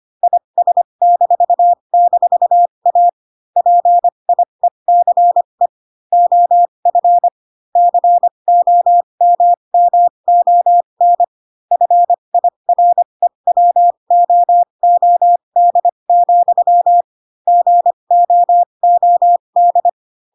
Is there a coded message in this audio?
Yes: Morse code from 0.2 to 19.9 s, heard as a keyed tone, about 25 wpm, spaced as for 22 wpm, 700 Hz. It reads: IS--A PIECE OF COMMON FIREWOOD, GOOD